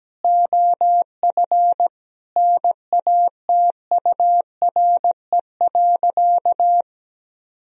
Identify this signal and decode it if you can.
Morse code, heard as a keyed tone, about 17 wpm, 700 Hz. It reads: OF NATURE.
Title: Sentences from Top 1000 Words - Review Part 3 - 17wpm